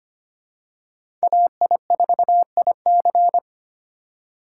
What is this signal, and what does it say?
Morse code, heard as a keyed tone, about 25 wpm, 700 Hz. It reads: AI4IC
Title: USA Call Signs - 5 Characters - 3x Character Spacing - 25wpm